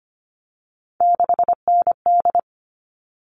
Morse code, heard as a keyed tone, about 25 wpm, 700 Hz. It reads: 6ND
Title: All Cut Numbers - 3-Digit 25wpm